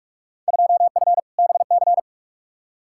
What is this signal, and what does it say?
Morse code, heard as a keyed tone, about 41 wpm, 700 Hz. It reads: 2F BC
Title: Field Day Exchanges Encore - 45wpm